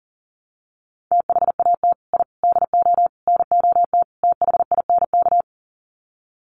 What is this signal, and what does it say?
Morse code, heard as a keyed tone, about 40 wpm, 700 Hz. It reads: THAT I DO NOT THINK
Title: Sets of 5 Words Encore 40wpm